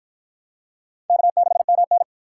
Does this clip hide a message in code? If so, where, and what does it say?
Yes: Morse code from 1.1 to 2.0 s, heard as a keyed tone, about 53 wpm, 700 Hz. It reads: K6MN